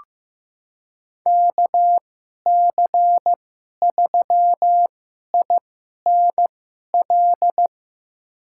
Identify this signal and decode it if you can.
Morse code, heard as a keyed tone, about 15 wpm, spaced as for 11 wpm, 700 Hz. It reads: KC3INL
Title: USA Call Signs - 2x Character Spacing - 15wpm